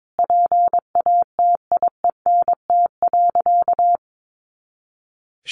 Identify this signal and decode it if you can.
Morse code, heard as a keyed tone, about 22 wpm, 700 Hz. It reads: PATIENT.
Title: Easy Phrases 22wpm